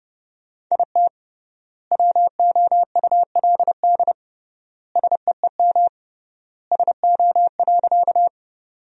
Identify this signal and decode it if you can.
Morse code, heard as a keyed tone, about 30 wpm, 700 Hz. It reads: IT WOULD SEEM SO.
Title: 3x Word-Spacing - Sentences from Top 300 Words - 30wpm